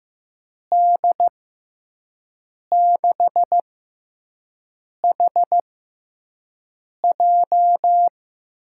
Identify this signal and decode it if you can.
Morse code, heard as a keyed tone, about 15 wpm, spaced as for 5 wpm, 700 Hz. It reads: D6HJ
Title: USA Call Signs - 6 Characters - 6x Character Spacing - 15wpm